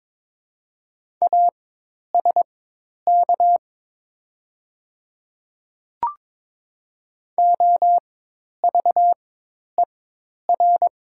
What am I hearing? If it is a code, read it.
Morse code, heard as a keyed tone, about 23 wpm, spaced as for 10 wpm, 700 Hz. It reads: ASK OVER
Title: 4x Character Spacing - Top 300 Words - 22wpm